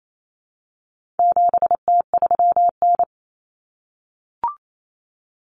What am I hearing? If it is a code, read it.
Morse code, heard as a keyed tone, about 28 wpm, 700 Hz. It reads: 7T3N
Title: Common Cut Numbers - 4-Digit 28wpm